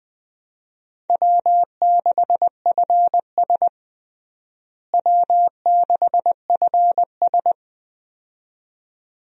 Morse code, heard as a keyed tone, about 21 wpm, 700 Hz. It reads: W6FS W6FS
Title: USA Call Signs - 4 Characters - Repeated 2x - 20wpm